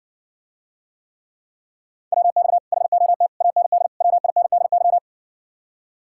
Morse code, heard as a keyed tone, about 60 wpm, 700 Hz. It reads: WX HOT AND WINDY